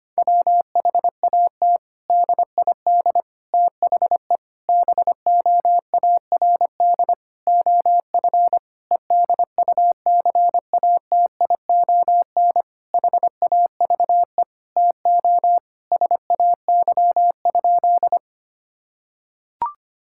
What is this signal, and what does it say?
Morse code, heard as a keyed tone, about 25 wpm, 700 Hz. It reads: WHAT DID THE BOARD OF EDUCATION HAVE TO SAY?
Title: Sentences from Top 1000 Words - Review Part 6 - 25wpm